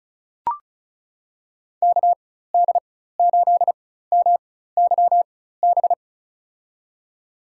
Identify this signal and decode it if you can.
Morse code, heard as a keyed tone, about 36 wpm, spaced as for 16 wpm, 700 Hz. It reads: KD8MYB
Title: USA Call Signs - 6 Characters - 4x Character Spacing - 35wpm